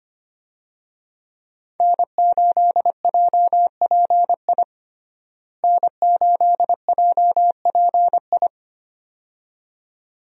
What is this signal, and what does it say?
Morse code, heard as a keyed tone, about 25 wpm, 700 Hz. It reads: N8JPI N8JPI
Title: USA Call Signs - 5 Characters - Repeated 2x - 25wpm